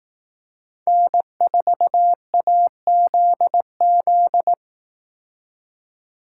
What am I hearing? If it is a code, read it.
Morse code, heard as a keyed tone, about 18 wpm, 700 Hz. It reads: N4AZZ
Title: Speed Racing - USA 1x3 Call Signs 15wpm